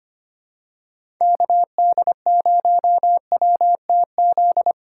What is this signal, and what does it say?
Morse code, heard as a keyed tone, about 25 wpm, 700 Hz. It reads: KD0WTZ
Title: USA Call Signs - 6 Characters - 4x Character Spacing - 25wpm